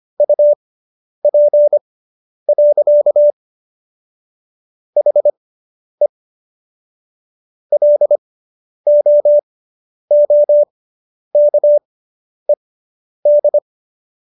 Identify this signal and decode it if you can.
Morse code, heard as a keyed tone, about 25 wpm, spaced as for 10 wpm, 590 Hz. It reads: UP. HE LOOKED